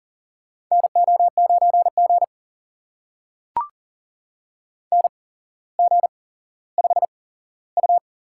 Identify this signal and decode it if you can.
Morse code, heard as a keyed tone, about 40 wpm, spaced as for 17 wpm, 700 Hz. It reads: NO9G NG5U
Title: USA Call Signs - 4 Characters - 8x Character Spacing - 40wpm